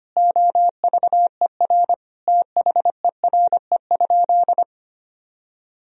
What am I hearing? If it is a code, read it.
Morse code, heard as a keyed tone, about 25 wpm, 700 Hz. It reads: OVER THERE?
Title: Sentences from Top 200 Words and Abbreviations 25wpm